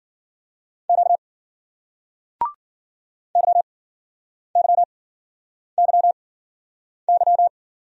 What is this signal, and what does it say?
Morse code, heard as a keyed tone, about 53 wpm, spaced as for 9 wpm, 700 Hz. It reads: Q YYYY